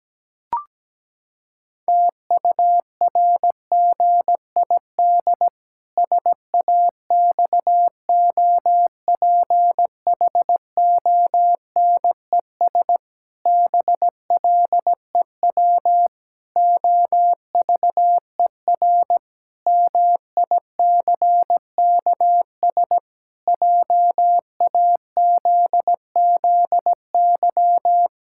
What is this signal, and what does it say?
Morse code, heard as a keyed tone, about 17 wpm, 700 Hz. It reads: TURGID SAXOPHONES BLEW OVER MICKS JAZZY
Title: Pangram Challenge v2 - 17wpm